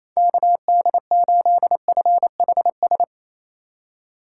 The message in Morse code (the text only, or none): KD8FHS